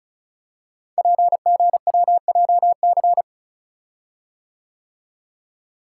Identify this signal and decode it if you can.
Morse code, heard as a keyed tone, about 36 wpm, 690 Hz. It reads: PGWJC
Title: Code Groups - 5 Letters - 35wpm